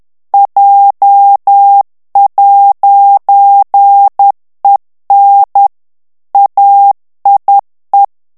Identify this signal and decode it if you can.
Morse code, heard as a keyed tone, about 10 wpm, 800 Hz. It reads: J'EN AIE
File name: FRANCAIS TEXTE CLAIR 2 VITESSE 10 MOTS MINUTES